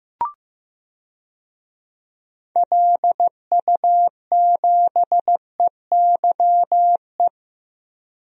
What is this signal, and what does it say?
Morse code, heard as a keyed tone, about 15 wpm, 700 Hz. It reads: LU7EYE
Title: DXCC-100 Call Signs - 6 Characters - 15wpm